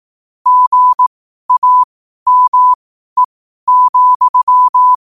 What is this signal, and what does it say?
Morse code, heard as a keyed tone, about 18 wpm, spaced as for 12 wpm, 1000 Hz. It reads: GAME,
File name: qotdfiles / 12wpm